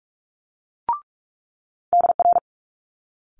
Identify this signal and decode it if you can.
Morse code, heard as a keyed tone, about 46 wpm, 700 Hz. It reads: DR
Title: Multi-letter Phonemes Quick Fire 45wpm